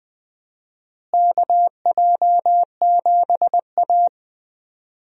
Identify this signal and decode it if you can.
Morse code, heard as a keyed tone, about 21 wpm, 700 Hz. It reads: KJ7A
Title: USA Call Signs - 4 Characters - Repeated 3x - 20wpm